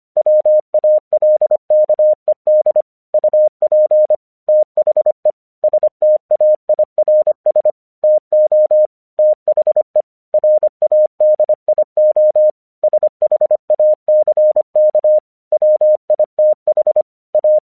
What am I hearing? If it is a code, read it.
Morse code, heard as a keyed tone, about 25 wpm, 610 Hz. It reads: WALKED UP THE STAIRS TO THE RADIO SHACK WITH A
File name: 1 English 25wpm / AQC08